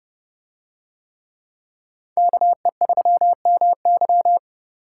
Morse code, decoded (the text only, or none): KE3MY